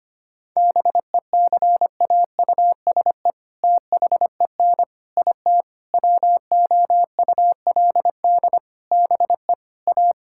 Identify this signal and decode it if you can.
Morse code, heard as a keyed tone, about 25 wpm, 700 Hz. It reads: BECAUSE THEN IT WOULD BE A
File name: practice / 25wpm / All-Age Appropriate Jokes - 25wpm